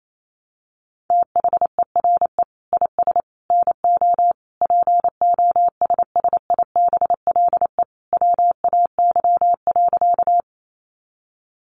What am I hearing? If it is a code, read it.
Morse code, heard as a keyed tone, about 28 wpm, 700 Hz. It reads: THERE IS NO POSSIBLE WAY.
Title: Sentences from Top 500 Words 28wpm